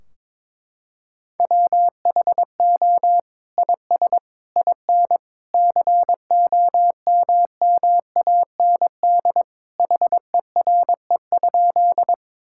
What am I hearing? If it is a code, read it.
Morse code, heard as a keyed tone, about 22 wpm, 700 Hz. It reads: WHO IS IN COMMAND HERE?
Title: Sentences from Top 1600 Words 22wpm